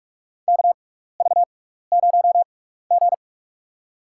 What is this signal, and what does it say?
Morse code, heard as a keyed tone, about 45 wpm, spaced as for 15 wpm, 700 Hz. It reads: KV0G
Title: USA Call Signs - 4 Characters - 6x Character Spacing - 45wpm